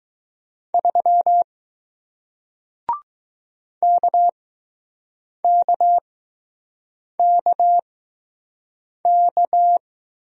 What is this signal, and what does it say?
Morse code, heard as a keyed tone, about 21 wpm, spaced as for 6 wpm, 700 Hz. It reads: 3 KKKK